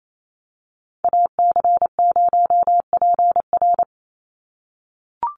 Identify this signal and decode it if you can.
Morse code, heard as a keyed tone, about 28 wpm, 700 Hz. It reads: AC0PR
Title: NA SOTA Activator Callsigns 28wpm